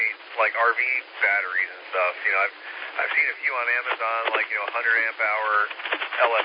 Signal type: Emergency Alert System (SAME) tones